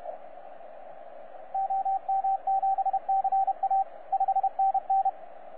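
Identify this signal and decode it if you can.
Morse code, heard as a keyed tone, about 31 wpm, 780 Hz, with radio noise. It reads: OM7CA 5NN
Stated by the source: 9U4U on 15 metre CW, DX station recording by DX World.net.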